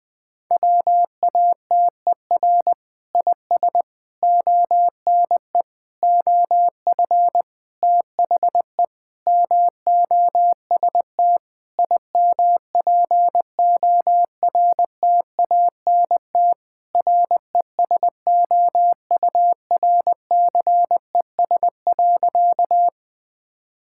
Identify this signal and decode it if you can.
Morse code, heard as a keyed tone, about 20 wpm, 700 Hz. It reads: WATER IS ONE OF THE MOST IMPORTANT RESOURCES.
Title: Sentences with 5-Word Phrases - 20wpm